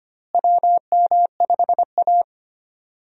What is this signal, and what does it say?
Morse code, heard as a keyed tone, about 25 wpm, 700 Hz. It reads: WM5A
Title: USA Call Signs - 4 Characters - Repeated 3x - 25wpm